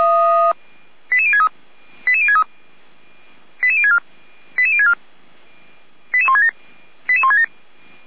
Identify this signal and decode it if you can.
Five-tone selective call (ZVEI), heard as quick runs of five tones, tones of about 70 ms. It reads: 88063 88063 88064 88064 88167 88167